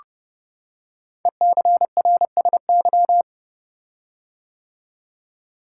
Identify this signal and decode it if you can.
Morse code, heard as a keyed tone, about 30 wpm, 700 Hz. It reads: ECRSY
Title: Code Groups - 5 Letters - 30wpm